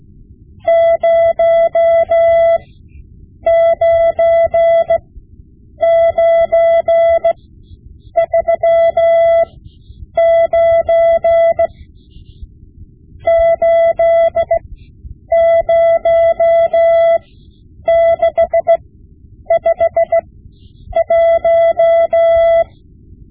Morse code, heard as a keyed tone, about 16 wpm, spaced as for 8 wpm, 670 Hz. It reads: 09939 80651
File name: Morse Lessons / Morse Lessons / Tape-5B